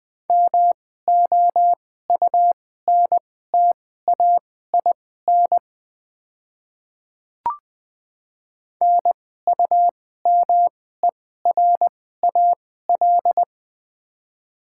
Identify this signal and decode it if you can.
Morse code, heard as a keyed tone, about 20 wpm, spaced as for 14 wpm, 700 Hz. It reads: MOUNTAIN NUMERAL
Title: 2x Character Spacing - Top 500 Words - 20wpm